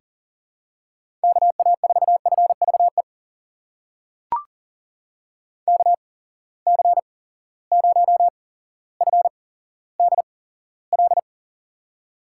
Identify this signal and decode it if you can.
Morse code, heard as a keyed tone, about 37 wpm, spaced as for 17 wpm, 700 Hz. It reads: KA4FVE KC0FDL
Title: USA Call Signs - 6 Characters - 8x Character Spacing - 40wpm